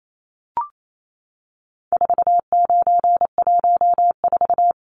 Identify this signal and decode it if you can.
Morse code, heard as a keyed tone, about 28 wpm, 700 Hz. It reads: 4914
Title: Numbers - 4-Digit 28wpm